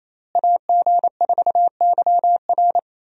Morse code, heard as a keyed tone, about 28 wpm, 700 Hz. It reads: AG4YR
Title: USA Call Signs - 5 Characters - Repeated 2x - 28wpm